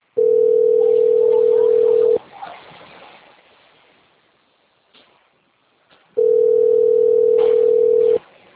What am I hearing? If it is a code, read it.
Ringback tone (North America), 440 + 480 Hz.